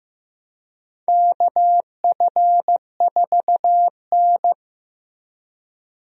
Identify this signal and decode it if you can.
Morse code, heard as a keyed tone, about 15 wpm, 700 Hz. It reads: KF4N
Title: USA Call Signs - 4 Characters - Repeated 2x - 15wpm